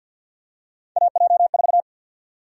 Morse code, heard as a keyed tone, about 47 wpm, 700 Hz. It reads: AJ4
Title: USA Call Sign Prefixes - 50wpm